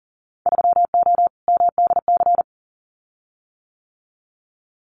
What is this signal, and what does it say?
Morse code, heard as a keyed tone, about 39 wpm, 700 Hz. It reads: 3O MDC
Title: Winter Field Day Exchanges 40wpm